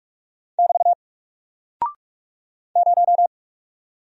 Morse code, heard as a keyed tone, about 44 wpm, 700 Hz. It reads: <BT> 0
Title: Single Character (Letter:Number:Punctuation:Prosign) Alt 45wpm